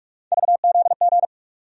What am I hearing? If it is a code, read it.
Morse code, heard as a keyed tone, about 47 wpm, 700 Hz. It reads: VZG
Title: USA Call Sign Postfixes - 3 Letters 45wpm